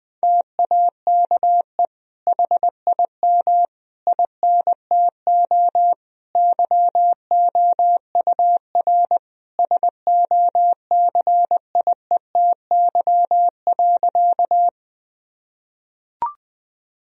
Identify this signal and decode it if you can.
Morse code, heard as a keyed tone, about 20 wpm, 700 Hz. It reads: TAKE HIM INTO YOUR SOCIETY.
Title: Sentences from Top 1000 Words - Review Part 4 - 20wpm